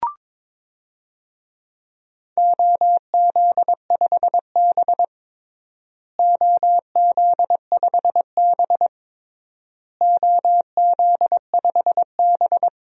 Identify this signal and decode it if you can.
Morse code, heard as a keyed tone, about 22 wpm, 700 Hz. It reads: OZ5B OZ5B OZ5B
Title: DXCC-100 Call Signs - 4 Characters - Repeated 3x - 22wpm